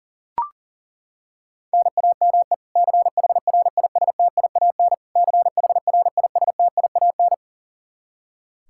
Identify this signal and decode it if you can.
Morse code, heard as a keyed tone, about 40 wpm, 700 Hz. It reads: NAME CHRISTIAN CHRISTIAN
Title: Practice QSO - Exchange 1 - Beginning - 40wpm